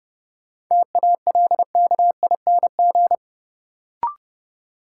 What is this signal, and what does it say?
Morse code, heard as a keyed tone, about 30 wpm, 700 Hz. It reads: TALKING
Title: Sets of 2 Words - New Word Drill - 30wpm